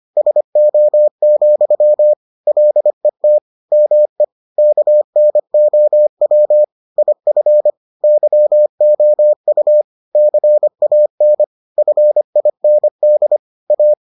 Morse code, heard as a keyed tone, about 25 wpm, 600 Hz. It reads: SO, LET ME KNOW IF YOU CAN FIND A